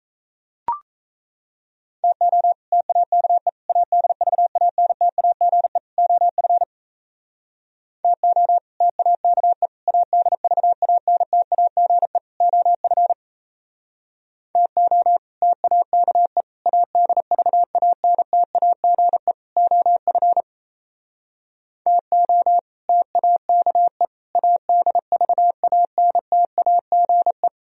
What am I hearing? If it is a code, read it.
Morse code, heard as a keyed tone, about 34 wpm, 690 Hz. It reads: TO TAKE ADVANTAGE OF TO TAKE ADVANTAGE OF TO TAKE ADVANTAGE OF TO TAKE ADVANTAGE
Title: Speed-Racing - Sets of 4 Words 28wpm